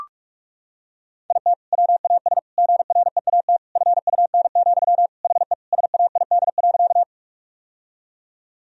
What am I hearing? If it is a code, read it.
Morse code, heard as a keyed tone, about 42 wpm, 700 Hz. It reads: IT WAS GREAT FUN, HE SAID.
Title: Sentences from Top 1000 Words - Review Part 4 - 45wpm